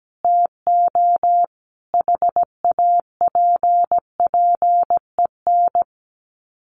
Morse code, heard as a keyed tone, about 17 wpm, 700 Hz. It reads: TO HAPPEN
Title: Sets of 5 Words Encore 17wpm